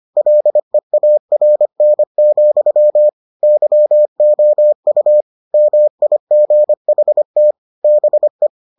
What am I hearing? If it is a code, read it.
Morse code, heard as a keyed tone, about 25 wpm, 600 Hz. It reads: LEARN, YOU MIGHT BE